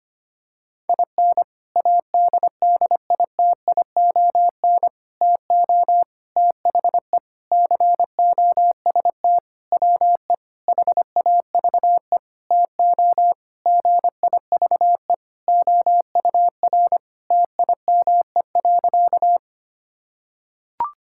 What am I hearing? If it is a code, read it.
Morse code, heard as a keyed tone, about 25 wpm, 700 Hz. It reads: IN ADDITION TO THE COST WE HAVE TO GIVE OUR TIME.
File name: practice / 25wpm / Sentences with 4-Word Phrases - 25wpm